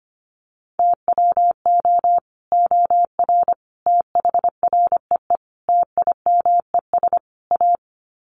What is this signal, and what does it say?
Morse code, heard as a keyed tone, about 25 wpm, 700 Hz. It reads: TWO OR THREE TIMES A
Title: Sets of 5 Words Encore2 25wpm